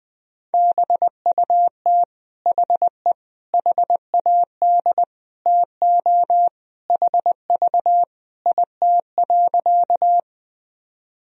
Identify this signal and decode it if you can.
Morse code, heard as a keyed tone, about 20 wpm, 700 Hz. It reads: BUT HE HAD TO HV IT.
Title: Sentences from Top 100 Words and Abbreviations 20wpm